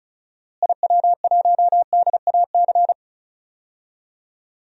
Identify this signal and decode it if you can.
Morse code, heard as a keyed tone, about 36 wpm, 700 Hz. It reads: IW1DAC
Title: DXCC-100 Call Signs - 6 Characters - Repeated 2x - 35wpm